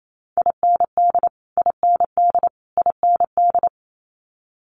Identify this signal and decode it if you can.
Morse code, heard as a keyed tone, about 28 wpm, 700 Hz. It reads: IND IND IND
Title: Multi-letter Phonemes Repeated 3x 28wpm